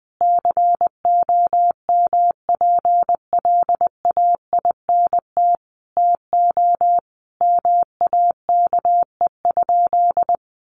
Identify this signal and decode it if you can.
Morse code, heard as a keyed tone, about 20 wpm, 700 Hz. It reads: COMPLAINT TO MAKE?
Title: Sentences from Top 1400 Words 20wpm